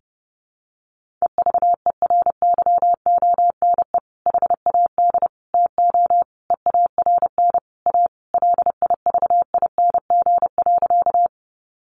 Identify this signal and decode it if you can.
Morse code, heard as a keyed tone, about 28 wpm, 700 Hz. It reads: EVERYONE HAD TO EARN A LIVING.
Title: Sentences from Top 1500 Words 30wpm